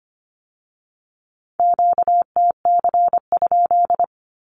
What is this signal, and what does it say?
Morse code, heard as a keyed tone, about 25 wpm, 700 Hz. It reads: QTC?
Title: Q-Codes with Spoken Meaning - 25wpm